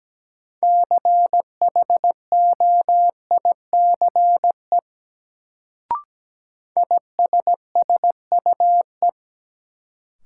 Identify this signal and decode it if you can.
Morse code, heard as a keyed tone, about 17 wpm, 700 Hz. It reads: CHOICE ISSUE
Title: Sentences from Top 900 Words - New Word Drill - 17wpm